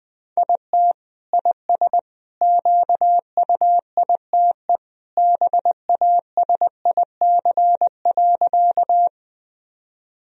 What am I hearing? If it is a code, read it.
Morse code, heard as a keyed tone, about 20 wpm, 700 Hz. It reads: IT IS QUITE BASIC.
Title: Sentences from Top 1000 Words - Review Part 3 - 20wpm